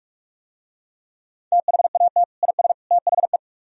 Morse code, heard as a keyed tone, about 44 wpm, 700 Hz. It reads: THAT IS THE